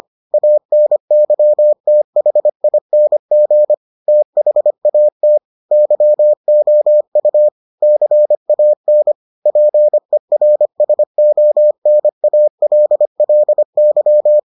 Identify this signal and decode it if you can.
Morse code, heard as a keyed tone, about 25 wpm, 600 Hz. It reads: ANYTHING THAT YOU CAN PERSONALLY